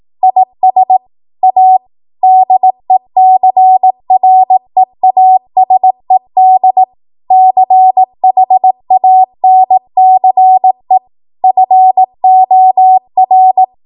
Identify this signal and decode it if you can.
Morse code, heard as a keyed tone, about 18 wpm, 750 Hz. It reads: IS A DECREASED CHANCE FOR
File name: Morse / 18WPM